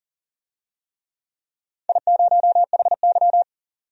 Morse code, heard as a keyed tone, about 39 wpm, 700 Hz. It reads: I0HY
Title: DXCC-100 Call Signs - 4 Characters - 40wpm